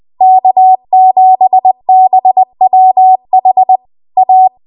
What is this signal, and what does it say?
Morse code, heard as a keyed tone, about 20 wpm, 750 Hz. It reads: K7BWH A
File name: Morse / 20WPM